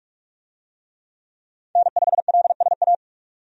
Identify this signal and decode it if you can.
Morse code, heard as a keyed tone, about 46 wpm, 700 Hz. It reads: N5LSA
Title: USA Call Signs - 5 Characters - 45wpm